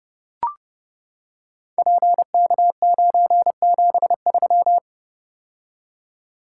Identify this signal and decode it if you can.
Morse code, heard as a keyed tone, about 31 wpm, 700 Hz. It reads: PK973
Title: Code Groups - 5 Letter:Numbers - 30wpm